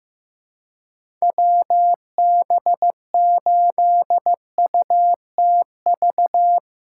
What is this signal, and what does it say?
Morse code, heard as a keyed tone, about 16 wpm, 700 Hz. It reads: WB8UTV
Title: USA Call Signs - 6 Characters - 3x Character Spacing - 15wpm